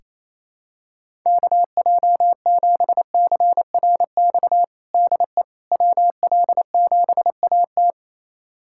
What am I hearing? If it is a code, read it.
Morse code, heard as a keyed tone, about 28 wpm, 700 Hz. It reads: KJ7CRX DE WL7AT
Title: USA Callsign DE Callsign 28wpm